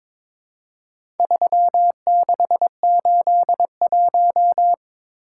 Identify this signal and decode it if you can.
Morse code, heard as a keyed tone, about 22 wpm, 700 Hz. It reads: 3681
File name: practice / 22wpm / Four Letter-Number Combinations - 22wpm